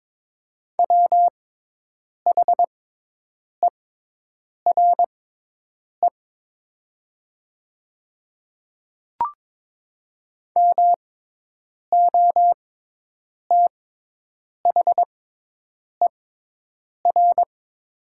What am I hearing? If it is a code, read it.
Morse code, heard as a keyed tone, about 22 wpm, spaced as for 8 wpm, 700 Hz. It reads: WHERE MOTHER
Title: 6x Character Spacing - Top 200 Words - 22wpm